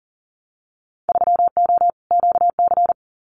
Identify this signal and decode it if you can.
Morse code, heard as a keyed tone, about 42 wpm, 700 Hz. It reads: 3O QC
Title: Winter Field Day Exchanges Encore 40wpm